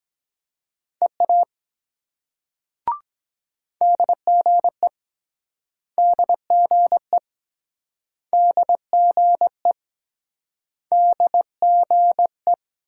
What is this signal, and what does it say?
Morse code, heard as a keyed tone, about 23 wpm, 700 Hz. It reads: EA DGE DGE DGE DGE